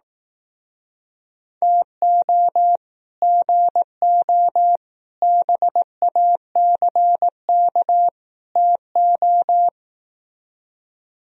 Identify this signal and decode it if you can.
Morse code, heard as a keyed tone, about 18 wpm, 700 Hz. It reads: TO GO BACK TO
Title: Speed-Racing - Sets of 4 Words 15wpm